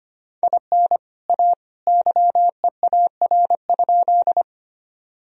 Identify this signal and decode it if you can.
Morse code, heard as a keyed tone, about 25 wpm, 700 Hz. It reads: IN A YEAR?
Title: Sentences from Top 1000 Words - Review Part 1 - 25wpm